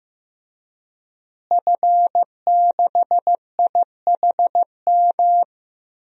Morse code, heard as a keyed tone, about 15 wpm, 700 Hz. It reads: F6IHM
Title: DXCC-100 Call Signs - 5 Characters - Repeated 3x - 15wpm